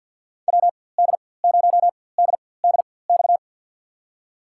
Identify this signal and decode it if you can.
Morse code, heard as a keyed tone, about 52 wpm, spaced as for 23 wpm, 700 Hz. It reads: WD0DDX
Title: USA Call Signs - 6 Characters - 4x Character Spacing - 50wpm